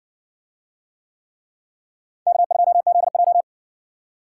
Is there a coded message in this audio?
Yes: Morse code from 2.3 to 3.4 s, heard as a keyed tone, about 60 wpm, 700 Hz. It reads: K2ZJ